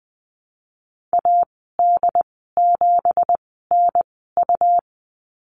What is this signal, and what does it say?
Morse code, heard as a keyed tone, about 20 wpm, spaced as for 15 wpm, 700 Hz. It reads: AD7NU
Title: 207 Lesson 33g - Letter j - Call Signs 20wpm